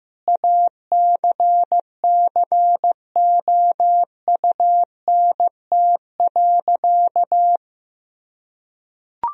Morse code, heard as a keyed tone, about 15 wpm, 700 Hz. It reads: ACCOUNT.
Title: Sentences from Top 1000 Words - Review Part 4 - 15wpm